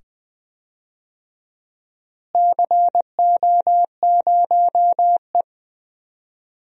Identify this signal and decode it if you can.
Morse code, heard as a keyed tone, about 20 wpm, 700 Hz. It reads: CO0E
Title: DXCC-100 Call Signs - 4 Characters - Repeated 2x - 20wpm